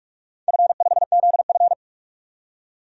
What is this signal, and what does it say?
Morse code, heard as a keyed tone, about 48 wpm, 700 Hz. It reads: F5ZF
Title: DXCC-100 Call Signs - 4 Characters - 45wpm